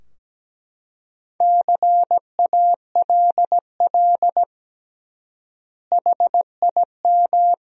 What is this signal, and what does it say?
Morse code, heard as a keyed tone, about 17 wpm, 700 Hz. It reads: CALL HIM
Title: 3x Word-Spacing - Sentences from Top 100 Words - 17wpm